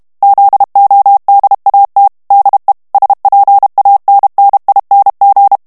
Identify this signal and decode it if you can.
Morse code, heard as a keyed tone, about 32 wpm, 800 Hz. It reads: ZODAT DE SPANNING
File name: HOLLANDAIS texte clair 2 vitesse 30 mots minute